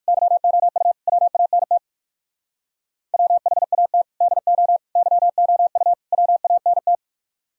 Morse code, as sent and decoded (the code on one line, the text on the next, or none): -.-- --- ..- / .-- .- -. - / .-- .... .- - / -.. --- / -.-- --- ..- / .-- .- -. -
YOU WANT WHAT DO YOU WANT